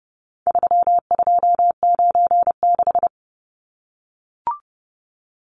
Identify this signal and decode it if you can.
Morse code, heard as a keyed tone, about 30 wpm, 700 Hz. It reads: 3296
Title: Numbers - 4-Digit 30wpm